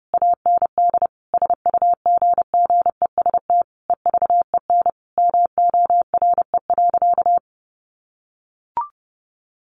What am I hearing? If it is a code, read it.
Morse code, heard as a keyed tone, about 30 wpm, 700 Hz. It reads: AND SUGGEST EVEN MORE.
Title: Sentences from Top 1300 Words 30wpm